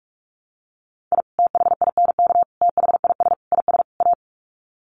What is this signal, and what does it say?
Morse code, heard as a keyed tone, about 44 wpm, 700 Hz. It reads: I THINK THIS IS A
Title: Sets of 5 Words 45wpm